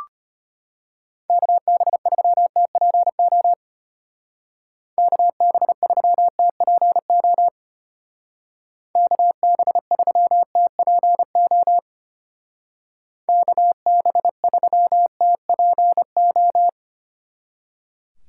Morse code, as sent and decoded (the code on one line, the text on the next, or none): -.- -... ...-- - .--. --- / -.- -... ...-- - .--. --- / -.- -... ...-- - .--. --- / -.- -... ...-- - .--. ---
KB3TPO KB3TPO KB3TPO KB3TPO